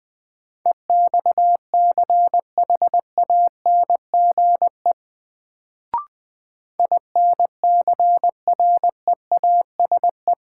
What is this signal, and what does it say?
Morse code, heard as a keyed tone, about 20 wpm, 700 Hz. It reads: EXCHANGE INCREASE
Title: Sets of 5 Words - New Word Drill - 20wpm